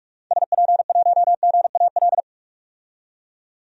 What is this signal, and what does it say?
Morse code, heard as a keyed tone, about 47 wpm, 700 Hz. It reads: SP1GAL
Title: DXCC-100 Call Signs - 6 Characters - 45wpm